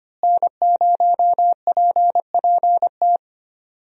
Morse code, heard as a keyed tone, about 25 wpm, 700 Hz. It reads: N0PPT
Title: USA Call Signs - 5 Characters - Repeated 3x - 25wpm